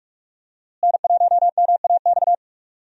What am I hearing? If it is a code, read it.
Morse code, heard as a keyed tone, about 41 wpm, 700 Hz. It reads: N1MAX
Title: USA Call Signs - 5 Characters - Repeated 3x - 45wpm